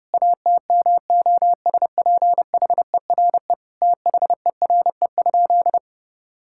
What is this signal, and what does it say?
Morse code, heard as a keyed tone, about 30 wpm, 700 Hz. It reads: ATMOSPHERE THERE?
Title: Sentences from Top 1600 Words 30wpm